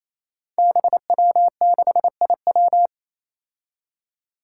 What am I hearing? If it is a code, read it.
Morse code, heard as a keyed tone, about 28 wpm, 700 Hz. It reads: BW6IW
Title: Five Challenging Letter-Number Combinations 28wpm